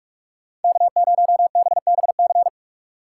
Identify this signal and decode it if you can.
Morse code, heard as a keyed tone, about 46 wpm, 700 Hz. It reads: K0BBC
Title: USA Call Signs - 5 Characters - 3x Character Spacing - 45wpm